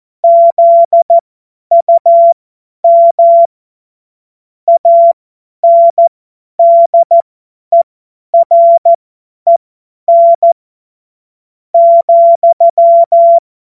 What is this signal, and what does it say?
Morse code, heard as a keyed tone, about 14 wpm, spaced as for 10 wpm, 680 Hz. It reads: ZUM ANDEREN ,
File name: Morsekurs / Witze / Witz 20. Juni 2023 50 F70